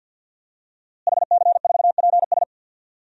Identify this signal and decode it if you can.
Morse code, heard as a keyed tone, about 52 wpm, 700 Hz. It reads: HC4PS
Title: DXCC-100 Call Signs - 5 Characters - Repeated 3x - 50wpm